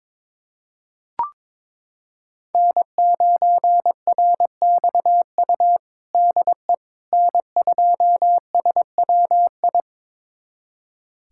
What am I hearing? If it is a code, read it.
Morse code, heard as a keyed tone, about 22 wpm, 700 Hz. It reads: N9RXU DE N2SWI
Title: USA Callsign DE Callsign - 5 Characters - 22wpm